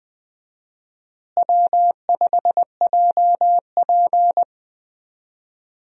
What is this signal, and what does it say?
Morse code, heard as a keyed tone, about 20 wpm, 700 Hz. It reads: W5JP
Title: USA Call Signs - 4 Characters - Repeated 3x - 20wpm